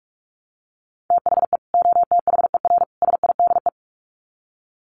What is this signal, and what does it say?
Morse code, heard as a keyed tone, about 44 wpm, 700 Hz. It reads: THE OTHER SIDE
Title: Sets of 3 Words 45wpm